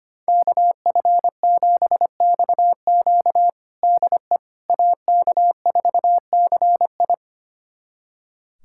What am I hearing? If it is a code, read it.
Morse code, heard as a keyed tone, about 25 wpm, 700 Hz. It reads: KF7XQ DE AK4CI
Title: USA Callsign DE Callsign - 5 Characters - 25wpm